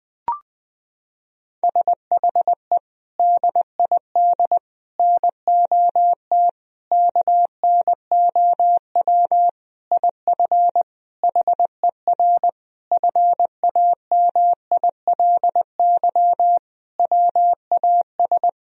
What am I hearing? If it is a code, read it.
Morse code, heard as a keyed tone, about 20 wpm, 700 Hz. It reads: SHE DID NOT KNOW IF HER FAMILY WAS
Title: Sentences from Top 1000 Words - Review Part 4 - 20wpm